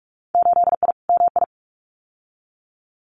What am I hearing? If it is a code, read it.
Morse code, heard as a keyed tone, about 46 wpm, 700 Hz. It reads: 8I MI
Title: Winter Field Day Exchanges Encore 45wpm